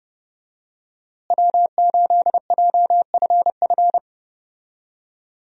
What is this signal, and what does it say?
Morse code, heard as a keyed tone, about 30 wpm, 700 Hz. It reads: W8JFF
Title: USA Call Signs - 5 Characters - Repeated 3x - 30wpm